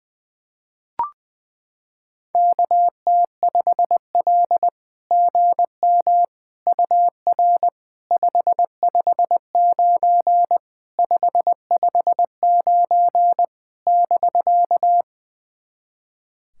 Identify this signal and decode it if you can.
Morse code, heard as a keyed tone, about 20 wpm, 700 Hz. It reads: KT5L GM UR 559 559 <BK>
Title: SOTA QSO Practice 20wpm